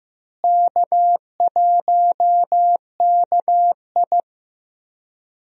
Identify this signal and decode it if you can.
Morse code, heard as a keyed tone, about 15 wpm, 700 Hz. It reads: K1KI
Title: USA Call Signs - 4 Characters - Repeated 2x - 15wpm